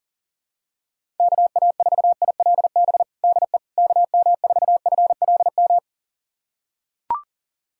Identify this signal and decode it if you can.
Morse code, heard as a keyed tone, about 40 wpm, 700 Hz. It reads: KA4ILB DE KM4FLM
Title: USA Callsign DE Callsign 40wpm